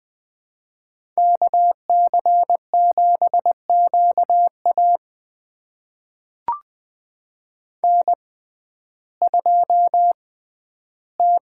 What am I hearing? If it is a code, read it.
Morse code, heard as a keyed tone, about 20 wpm, spaced as for 12 wpm, 700 Hz. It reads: KC7QA N2T
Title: USA Call Signs - 6x Character Spacing - 20wpm